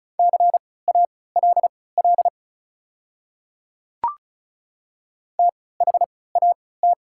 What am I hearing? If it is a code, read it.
Morse code, heard as a keyed tone, about 34 wpm, spaced as for 20 wpm, 700 Hz. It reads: CALL THAT
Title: 3x Character Spacing - Top 200 Words - 35wpm